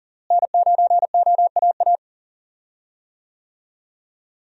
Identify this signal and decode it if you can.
Morse code, heard as a keyed tone, about 40 wpm, 700 Hz. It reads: N9OAA
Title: USA Call Signs - 5 Characters - Repeated 2x - 40wpm